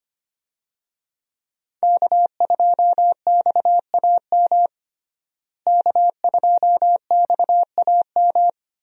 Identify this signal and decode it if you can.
Morse code, heard as a keyed tone, about 25 wpm, 700 Hz. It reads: K2XAM K2XAM
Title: USA Call Signs - 5 Characters - Repeated 3x - 25wpm